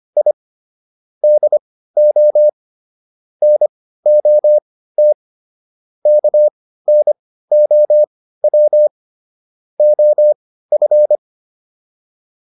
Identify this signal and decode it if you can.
Morse code, heard as a keyed tone, about 25 wpm, spaced as for 15 wpm, 600 Hz. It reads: I DO NOT KNOW OF